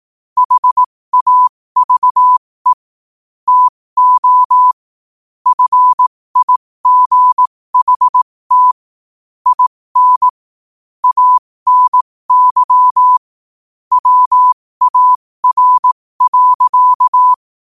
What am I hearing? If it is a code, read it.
Morse code, heard as a keyed tone, about 18 wpm, spaced as for 15 wpm, 1000 Hz. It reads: HAVE TO FIGHT IN ANY WAR.
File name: qotdfiles / 15wpm